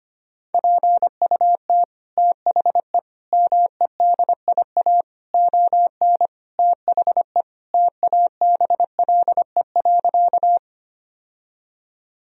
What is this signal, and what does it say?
Morse code, heard as a keyed tone, about 25 wpm, 700 Hz. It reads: PUT THE MEDIA ON THE TABLE.
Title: Sentences from Top 1000 Words - Review Part 4 - 25wpm